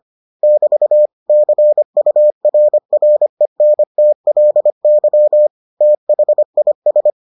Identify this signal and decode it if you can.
Morse code, heard as a keyed tone, about 25 wpm, 600 Hz. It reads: <BT> CURRENTLY THIS